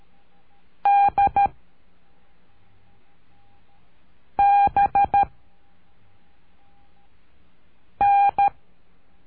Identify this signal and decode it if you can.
Morse code, heard as a keyed tone, about 13 wpm, 800 Hz. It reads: D B N
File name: cursos / CURSO DE TELEGRAFIA / Telegrafia - Trilha 03